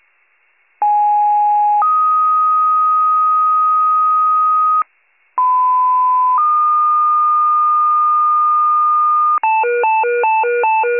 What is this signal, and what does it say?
Two-tone page, heard as a one-second tone, then a three-second tone, about 830 then 1250 Hz, about 1010 then 1220 Hz. It is heard from 0.8 to 9.4 s.
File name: pager / 94 RESCUE OFC / wcalert